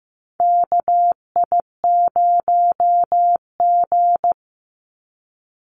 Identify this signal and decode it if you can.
Morse code, heard as a keyed tone, about 15 wpm, 700 Hz. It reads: KI0G
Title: NA SOTA Chaser Callsigns 15wpm